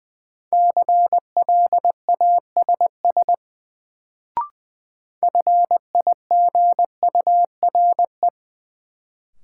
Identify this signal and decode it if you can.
Morse code, heard as a keyed tone, about 20 wpm, 700 Hz. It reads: CLASS FIGURE
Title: Sentences from Top 1000 Words - Review Part 2 - New Word Drill - 20wpm